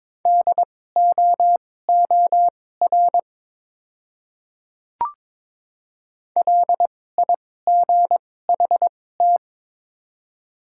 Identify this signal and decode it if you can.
Morse code, heard as a keyed tone, about 22 wpm, spaced as for 16 wpm, 700 Hz. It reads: DOOR LIGHT
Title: 2x Character Spacing - Top 300 Words - 22wpm